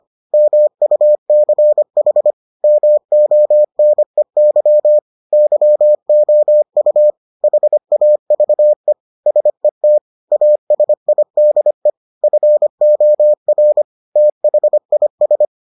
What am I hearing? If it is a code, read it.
Morse code, heard as a keyed tone, about 25 wpm, 600 Hz. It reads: MUCH MONEY YOU HAVE SET ASIDE FOR THIS